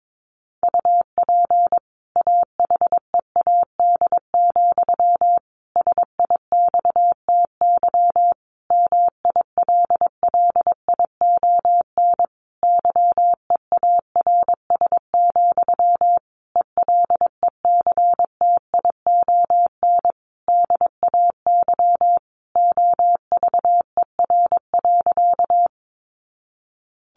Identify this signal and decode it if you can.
Morse code, heard as a keyed tone, about 22 wpm, 700 Hz. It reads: UP AHEAD, SIXTY MILLION YEARS, ELECTION DAY OVER.
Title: A Sound Of Thunder 22wpm